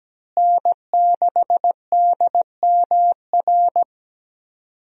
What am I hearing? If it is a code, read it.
Morse code, heard as a keyed tone, about 17 wpm, 700 Hz. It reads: N6DMR
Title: USA Call Signs - 5 Characters - Repeated 3x - 17wpm